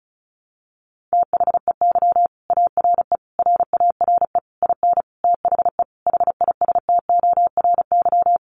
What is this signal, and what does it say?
Morse code, heard as a keyed tone, about 34 wpm, 700 Hz. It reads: THEY ARE RARE IN THE HISTORY
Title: Sentences from Top 1100 Words 35wpm